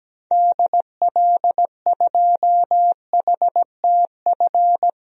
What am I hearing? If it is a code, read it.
Morse code, heard as a keyed tone, about 17 wpm, 700 Hz. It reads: DL2HTF
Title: DXCC-100 Call Signs - 6 Characters - 17wpm